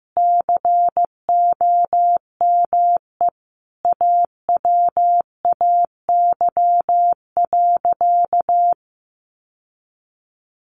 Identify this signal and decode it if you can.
Morse code, heard as a keyed tone, about 15 wpm, 700 Hz. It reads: COME AWAY.
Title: Sentences from Top 1000 Words 15wpm